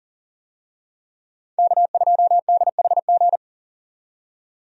Morse code, heard as a keyed tone, about 37 wpm, 700 Hz. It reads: K2DHG
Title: USA Call Signs - 5 Characters - 40wpm